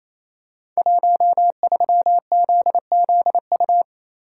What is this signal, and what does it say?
Morse code, heard as a keyed tone, about 28 wpm, 700 Hz. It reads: 13ZZU
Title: Code Groups - 5 Letter:Numbers - 28wpm